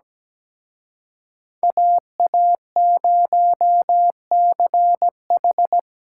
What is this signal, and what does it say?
Morse code, heard as a keyed tone, about 20 wpm, 700 Hz. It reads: AA0CH
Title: Speed Racing - USA 2x2 Call Signs 17wpm